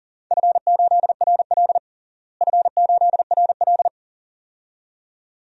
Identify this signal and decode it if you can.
Morse code, heard as a keyed tone, about 40 wpm, 690 Hz. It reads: F8RL F8RL
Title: DXCC-100 Call Signs - 4 Characters - Repeated 2x - 40wpm